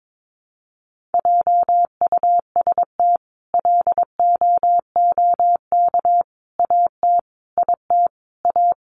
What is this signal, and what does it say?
Morse code, heard as a keyed tone, about 22 wpm, 700 Hz. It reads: JUST LOOK AT IT A
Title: Sentences from Top 1000 Words 22wpm